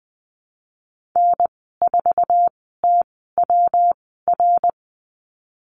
Morse code, heard as a keyed tone, about 20 wpm, spaced as for 15 wpm, 700 Hz. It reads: N4TWR
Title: 122 Lesson 20g - Number 3 - Call Signs 20wpm